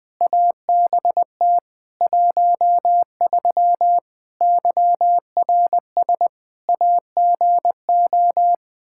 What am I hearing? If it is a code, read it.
Morse code, heard as a keyed tone, about 20 wpm, 700 Hz. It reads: ABT 13 YRS AGO
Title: Practice QSO - Exchange 3 - More Info - 20wpm